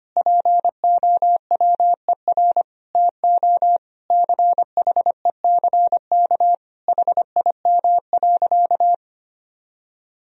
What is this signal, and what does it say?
Morse code, heard as a keyed tone, about 25 wpm, 700 Hz. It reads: POWER TO CHECK HIM.
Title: Sentences from Top 1000 Words - Review Part 3 - 25wpm